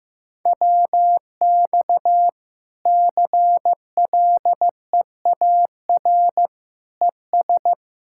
Morse code, heard as a keyed tone, about 15 wpm, 700 Hz. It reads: WX CLEAR ES
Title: Practice QSO - Exchange 2 - Info - 15wpm